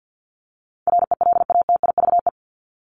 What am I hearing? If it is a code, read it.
Morse code, heard as a keyed tone, about 50 wpm, 700 Hz. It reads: RELATIVE
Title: Top 1251-1500 Words 50wpm